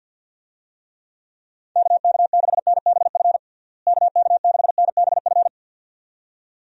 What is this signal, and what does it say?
Morse code, heard as a keyed tone, about 51 wpm, 700 Hz. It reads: KK6NBF KK6NBF
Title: USA Call Signs - 6 Characters - Repeated 2x - 50wpm